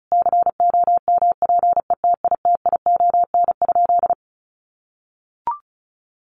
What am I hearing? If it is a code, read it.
Morse code, heard as a keyed tone, about 34 wpm, 700 Hz. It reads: COMPETITION?
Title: Sentences from Top 1000 Words 35wpm